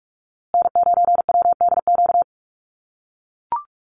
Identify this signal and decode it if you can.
Morse code, heard as a keyed tone, about 44 wpm, 690 Hz. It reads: N9WDQ
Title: USA Call Signs Repeated 2x 45wpm